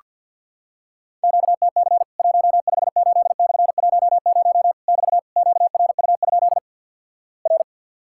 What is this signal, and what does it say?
Morse code, heard as a keyed tone, about 48 wpm, 700 Hz. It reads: QTC 158/10 <BT> QRU? R
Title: Worked All Europe DX Contest - QTC Exchange - 50wpm